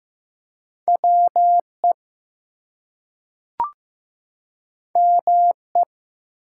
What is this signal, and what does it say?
Morse code, heard as a keyed tone, about 15 wpm, 700 Hz. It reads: WE ME
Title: Two Letter Words - 15wpm